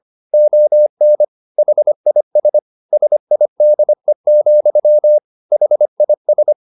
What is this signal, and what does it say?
Morse code, heard as a keyed tone, about 25 wpm, 600 Hz. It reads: ON HIS SIDE, HIS